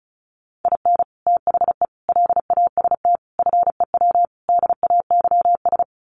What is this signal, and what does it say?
Morse code, heard as a keyed tone, about 34 wpm, 700 Hz. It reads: IN THE LAST FEW DAYS
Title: Sets of 5 Words Encore2 35wpm